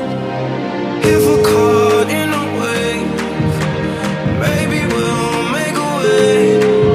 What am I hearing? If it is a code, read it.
Ringback tone (Europe, 425 Hz), with noise.